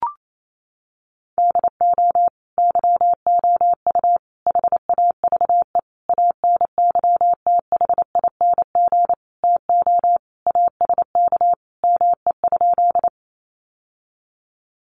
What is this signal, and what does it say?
Morse code, heard as a keyed tone, about 28 wpm, 700 Hz. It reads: DO YOU HAVE ANYTHING TO ASK ME?
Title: Sentences from Top 1000 Words 28wpm